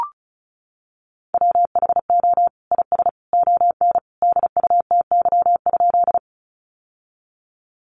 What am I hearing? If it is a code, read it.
Morse code, heard as a keyed tone, about 34 wpm, 700 Hz. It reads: WHO IS ON DUTY?
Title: Sentences from Top 1500 Words 35wpm